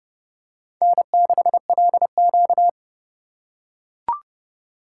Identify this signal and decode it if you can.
Morse code, heard as a keyed tone, about 30 wpm, 700 Hz. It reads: N6LQ
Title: USA Call Signs - 4 Characters - 2x Character Spacing - 30wpm